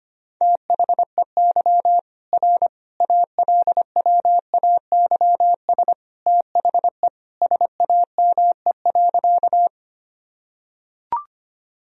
Morse code, heard as a keyed tone, about 25 wpm, 700 Hz. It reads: THEY R ALWAYS THE SAME.
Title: Sentences from Top 300 Words and Abbreviations 25wpm